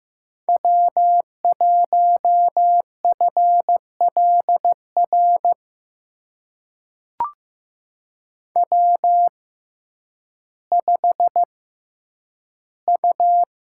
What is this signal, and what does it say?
Morse code, heard as a keyed tone, about 15 wpm, spaced as for 9 wpm, 700 Hz. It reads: W1FLR W5U
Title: USA Call Signs - 5 Characters - 6x Character Spacing - 15wpm